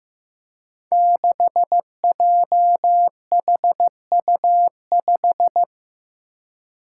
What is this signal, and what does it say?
Morse code, heard as a keyed tone, about 15 wpm, 700 Hz. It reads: 6JHU5
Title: Five Challenging Letter-Number Combinations 15wpm